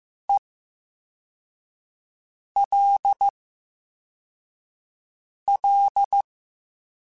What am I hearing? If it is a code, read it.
Morse code, heard as a keyed tone, about 15 wpm, spaced as for 4 wpm, 800 Hz. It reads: ELL
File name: Special INGs-mp3s / Special INGs-mp3s / VerySlowToFaster Ing @13